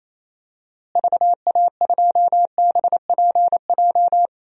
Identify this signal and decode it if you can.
Morse code, heard as a keyed tone, about 28 wpm, 700 Hz. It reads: VA2BPJ
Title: DXCC-100 Call Signs - 6 Characters - Repeated 3x - 28wpm